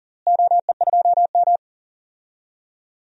Morse code, heard as a keyed tone, about 40 wpm, 700 Hz. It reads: OE2M
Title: DXCC-100 Call Signs - 4 Characters - Repeated 3x - 40wpm